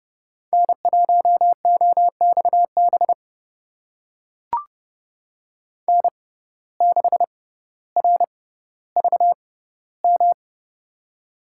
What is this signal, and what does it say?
Morse code, heard as a keyed tone, about 30 wpm, spaced as for 15 wpm, 700 Hz. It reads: N1OXB N6RVM
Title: USA Call Signs - 5 Characters - 6x Character Spacing - 30wpm